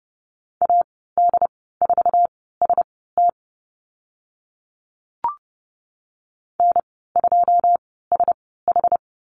Morse code, heard as a keyed tone, about 30 wpm, spaced as for 17 wpm, 700 Hz. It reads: AD4ST N2SH
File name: Lesson 15g - Number 5 - Call Signs 30wpm